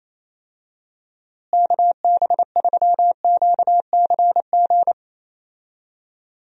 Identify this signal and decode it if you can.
Morse code, heard as a keyed tone, about 28 wpm, 700 Hz. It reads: KB3QCG
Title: USA Call Signs - 6 Characters - Repeated 3x - 28wpm